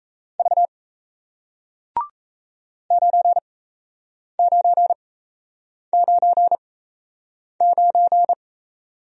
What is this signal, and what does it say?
Morse code, heard as a keyed tone, about 37 wpm, spaced as for 8 wpm, 700 Hz. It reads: V 9999